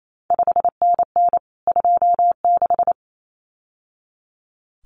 Morse code, heard as a keyed tone, about 28 wpm, 700 Hz. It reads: 5NN 26
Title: CQ WW DX Contest Exchange Encore - 28wpm